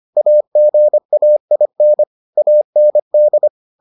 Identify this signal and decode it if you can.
Morse code, heard as a keyed tone, about 25 wpm, 600 Hz. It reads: AGAIN AND